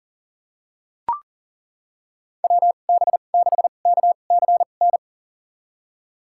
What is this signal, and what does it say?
Morse code, heard as a keyed tone, about 40 wpm, spaced as for 29 wpm, 700 Hz. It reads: WB6KCN
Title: USA Call Signs - 6 Characters - 2x Character Spacing - 40wpm